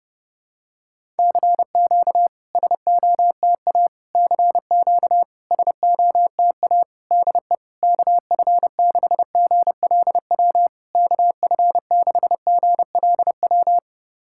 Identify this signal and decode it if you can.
Morse code, heard as a keyed tone, about 32 wpm, 690 Hz. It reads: CQ SOTA CQ SOTA DE KF6GLW KF6GLW
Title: SOTA QSO Practice 30wpm